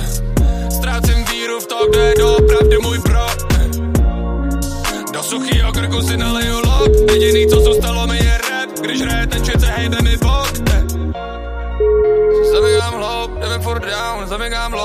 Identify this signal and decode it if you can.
Ringback tone (Europe, 425 Hz), with noise.